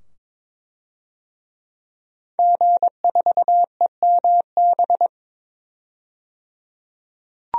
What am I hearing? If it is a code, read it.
Morse code, heard as a keyed tone, about 23 wpm, 700 Hz. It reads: G4EMB
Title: DXCC-100 Call Signs - 5 Characters - 22wpm